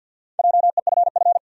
Morse code, heard as a keyed tone, about 46 wpm, 700 Hz. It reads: JEFF
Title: USA Common Names - Max 4 Letters - 50wpm